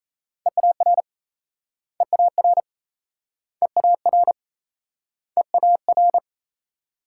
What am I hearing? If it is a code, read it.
Morse code, heard as a keyed tone, about 37 wpm, 700 Hz. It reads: EAR EAR EAR EAR